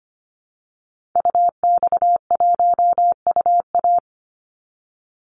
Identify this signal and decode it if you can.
Morse code, heard as a keyed tone, about 25 wpm, 700 Hz. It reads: UX1UA
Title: Call Signs - 25wpm